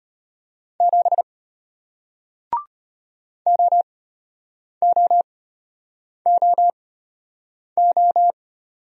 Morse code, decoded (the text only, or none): 7 OOOO